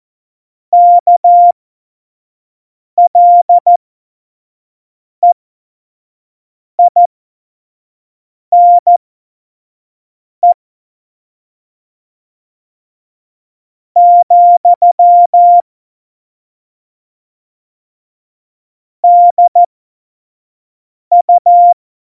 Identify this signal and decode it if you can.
Morse code, heard as a keyed tone, about 14 wpm, spaced as for 4 wpm, 700 Hz. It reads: KLEINE, DU
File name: Morsekurs / Witze / Witz 19. Februar 2019, 25 F70